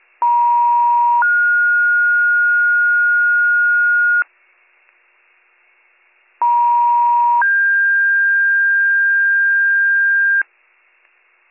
Two-tone page, heard as a one-second tone, then a three-second tone, about 950 then 1430 Hz, about 950 then 1600 Hz, with noise. It is heard from 0.2 to 10.4 s.